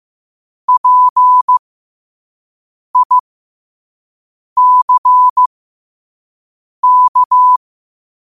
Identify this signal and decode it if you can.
Morse code, heard as a keyed tone, about 15 wpm, spaced as for 5 wpm, 1000 Hz. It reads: PICK
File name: qotdfiles / 5wpm